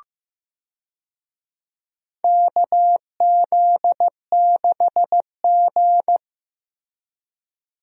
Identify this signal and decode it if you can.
Morse code, heard as a keyed tone, about 15 wpm, 700 Hz. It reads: KZ6G
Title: USA Call Signs - 4 Characters - Repeated 2x - 15wpm